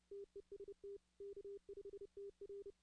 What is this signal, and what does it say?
Morse code, heard as a keyed tone, about 30 wpm, 400 Hz. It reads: TEST K5TR
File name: 2016-ARRL-DX-CW